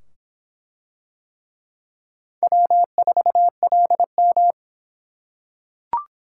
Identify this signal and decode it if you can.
Morse code, heard as a keyed tone, about 25 wpm, 700 Hz. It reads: W4LM